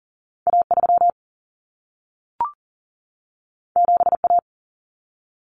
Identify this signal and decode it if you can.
Morse code, heard as a keyed tone, about 40 wpm, 700 Hz. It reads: A3 7A
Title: Common Cut Numbers - 2-Digit 40wpm